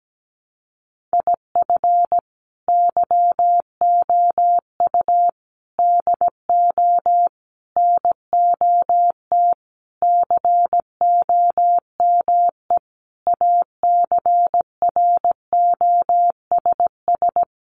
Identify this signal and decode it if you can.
Morse code, heard as a keyed tone, about 17 wpm, 700 Hz. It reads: IF YOU DO NOT COME ACROSS
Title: Sentences from Top 800 Words 17wpm